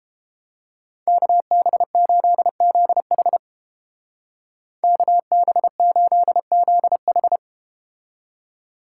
Morse code, heard as a keyed tone, about 33 wpm, 700 Hz. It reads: KB8ZH KB8ZH